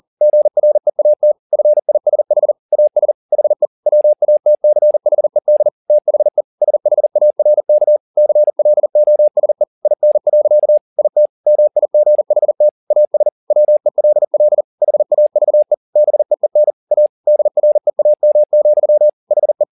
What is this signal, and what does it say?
Morse code, heard as a keyed tone, about 40 wpm, 600 Hz. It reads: GREAT FISH AS HE WATCHED THE SHARK CLOSE IN. IT MIGHT AS WELL HAVE BEEN A DREAM, HE
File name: the old man and the sea 2 40wpm